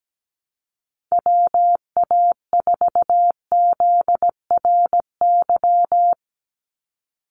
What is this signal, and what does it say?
Morse code, heard as a keyed tone, about 17 wpm, 700 Hz. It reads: WA4ZRY
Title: USA Call Signs 17wpm